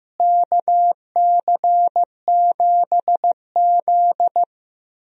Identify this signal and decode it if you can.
Morse code, heard as a keyed tone, about 15 wpm, 700 Hz. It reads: KC7Z
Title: USA Call Signs - 4 Characters - Repeated 2x - 15wpm